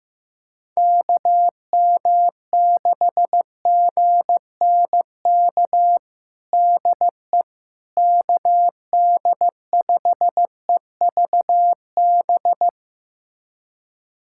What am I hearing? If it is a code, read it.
Morse code, heard as a keyed tone, about 16 wpm, 700 Hz. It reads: KM6GNK DE KD5EVB
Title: USA Callsign DE Callsign - 6 Characters - 15wpm